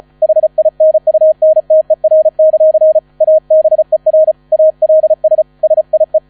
Morse code, heard as a keyed tone, about 36 wpm, 630 Hz. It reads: HINUNTER; ABER ALS SIE
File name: 35wpm